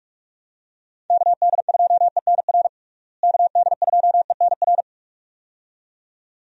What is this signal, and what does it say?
Morse code, heard as a keyed tone, about 46 wpm, 700 Hz. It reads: KD2ENR KD2ENR
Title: USA Call Signs - 6 Characters - Repeated 2x - 45wpm